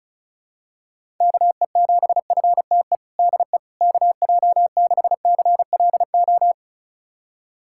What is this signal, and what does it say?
Morse code, heard as a keyed tone, about 36 wpm, 700 Hz. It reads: KE7FTE DE KJ6CLO
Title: USA Callsign DE Callsign - 6 Characters - 35wpm